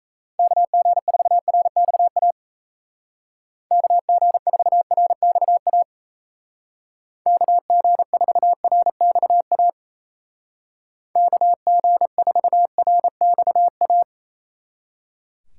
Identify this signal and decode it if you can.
Morse code, heard as a keyed tone, about 35 wpm, 700 Hz. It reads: KG4RXA KG4RXA KG4RXA KG4RXA